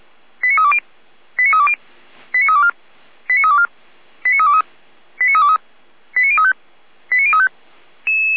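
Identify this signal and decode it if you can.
Five-tone selective call (ZVEI), heard as quick runs of five tones, tones of about 70 ms. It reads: 89320 89320 89324 89324 89323 89323 89035 89035
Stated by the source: ZVEI five-tone selective call received on SDR.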